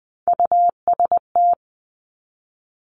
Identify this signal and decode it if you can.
Morse code, heard as a keyed tone, about 20 wpm, 700 Hz. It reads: UST
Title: Three Challenging Letter-Number Combinations 20wpm